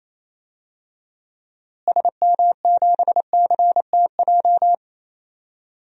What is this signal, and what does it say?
Morse code, heard as a keyed tone, about 28 wpm, 700 Hz. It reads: SM7CTJ
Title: DXCC-100 Call Signs - 6 Characters - Repeated 2x - 28wpm